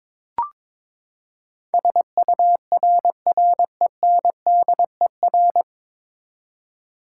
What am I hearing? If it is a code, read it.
Morse code, heard as a keyed tone, about 22 wpm, 700 Hz. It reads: SURRENDER
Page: Morse Code Ninja practice 22wpm